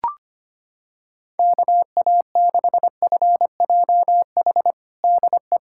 Morse code, heard as a keyed tone, about 25 wpm, 700 Hz. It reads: KA6FJH DE
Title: USA Callsign DE Callsign - 6 Characters - 25wpm